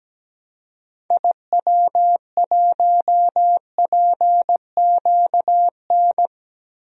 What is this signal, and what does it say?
Morse code, heard as a keyed tone, about 17 wpm, 700 Hz. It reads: IW1PQN
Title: DXCC-100 Call Signs - 6 Characters - Repeated 2x - 17wpm